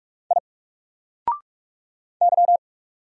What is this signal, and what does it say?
Morse code, heard as a keyed tone, about 44 wpm, 700 Hz. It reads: I Y
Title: Single Character (Letter:Number:Punctuation:Prosign) Alt 45wpm